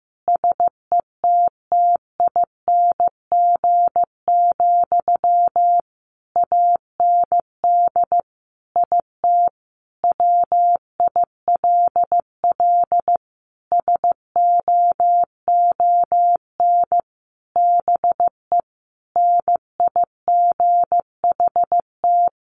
Morse code, heard as a keyed tone, about 15 wpm, 700 Hz. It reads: SETTING, AND IT WILL SOON BE NIGHT
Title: Sentences from Top 1300 Words 15wpm